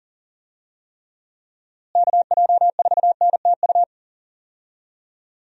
Morse code, heard as a keyed tone, about 40 wpm, 700 Hz. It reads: KJ4NTU
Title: USA Call Signs - 6 Characters - Repeated 3x - 40wpm